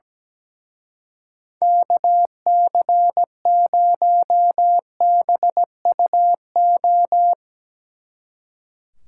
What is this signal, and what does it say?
Morse code, heard as a keyed tone, about 19 wpm, 700 Hz. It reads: KC0BUO
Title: Speed Racing - USA 2x3 Call Signs 17wpm